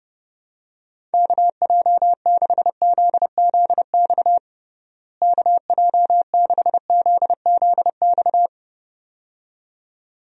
Morse code, heard as a keyed tone, about 30 wpm, 700 Hz. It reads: KJ6ZZX KJ6ZZX
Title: USA Call Signs - 6 Characters - Repeated 2x - 30wpm